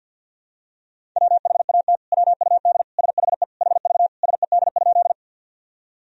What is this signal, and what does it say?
Morse code, heard as a keyed tone, about 50 wpm, 700 Hz. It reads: WHAT WUD SHE HV SED?
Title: Sentences from Top 100 Words and Abbreviations 50wpm